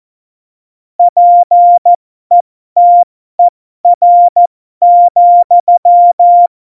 Morse code, heard as a keyed tone, about 14 wpm, spaced as for 12 wpm, 700 Hz. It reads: PETER,
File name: Morsekurs / Witze